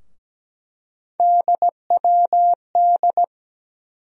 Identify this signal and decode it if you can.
Morse code, heard as a keyed tone, about 17 wpm, 700 Hz. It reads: DWD
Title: USA Call Sign Postfixes - 3 Letters 17wpm